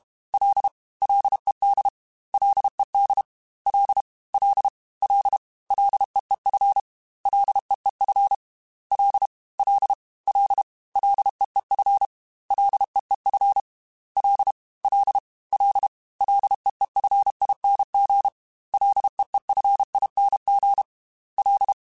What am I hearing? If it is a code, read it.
Morse code, heard as a keyed tone, about 32 wpm, 800 Hz. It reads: L LED LED L L L LEEF LEEF L L L LEEF LEEF L L L LEEFING LEEFING L
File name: Ls / Single L m3s / Single L @30